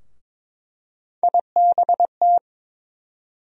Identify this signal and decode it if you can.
Morse code, heard as a keyed tone, about 22 wpm, 700 Hz. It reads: IBT
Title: USA Call Sign Postfixes - 3 Letters 22wpm